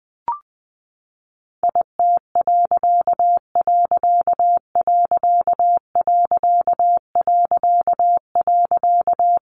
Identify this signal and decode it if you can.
Morse code, heard as a keyed tone, about 20 wpm, 700 Hz. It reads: IT......
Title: A Sound Of Thunder 20wpm